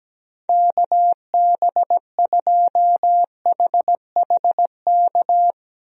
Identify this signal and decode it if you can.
Morse code, heard as a keyed tone, about 17 wpm, 700 Hz. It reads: KB2HHK
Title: USA Call Signs - 6 Characters - Repeated 2x - 17wpm